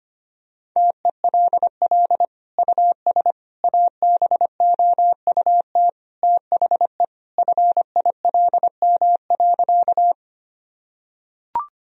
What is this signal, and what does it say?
Morse code, heard as a keyed tone, about 25 wpm, 700 Hz. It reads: TELL US ABOUT THE FILM.
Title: Sentences from Top 1000 Words - Review Part 3 - 25wpm